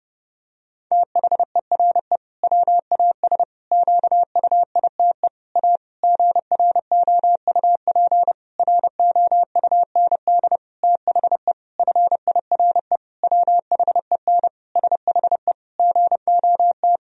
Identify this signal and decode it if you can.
Morse code, heard as a keyed tone, about 30 wpm, 700 Hz. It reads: THERE WAS QUITE A GROUP ROUND THE FIRE WHEN SHE GOT
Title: Sentences from Top 1000 Words - Review Part 1 - 30wpm